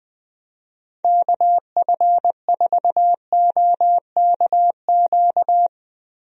Morse code, heard as a keyed tone, about 20 wpm, 700 Hz. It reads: KF4OKQ
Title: USA Call Signs - 6 Characters - Repeated 3x - 20wpm